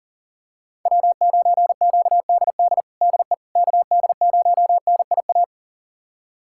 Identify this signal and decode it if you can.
Morse code, heard as a keyed tone, about 40 wpm, 700 Hz. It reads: W9QDD DE KD0NIA
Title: USA Callsign DE Callsign 40wpm